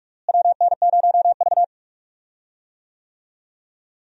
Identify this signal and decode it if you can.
Morse code, heard as a keyed tone, about 44 wpm, 700 Hz. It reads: WN0V
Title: USA Call Signs - 4 Characters - Repeated 2x - 45wpm